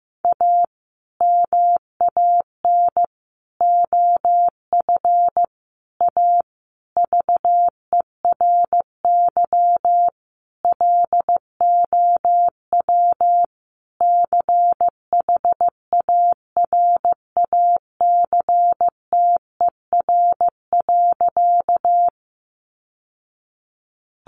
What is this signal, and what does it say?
Morse code, heard as a keyed tone, about 15 wpm, 700 Hz. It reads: A MAN OF A VERY LOW CHARACTER.
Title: Sentences from Top 1200 Words 15wpm